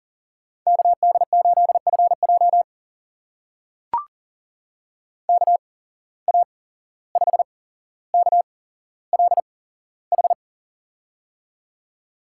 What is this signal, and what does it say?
Morse code, heard as a keyed tone, about 42 wpm, spaced as for 16 wpm, 700 Hz. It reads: KD8FJ KA5KLH
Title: USA Call Signs - 8x Character Spacing - 40wpm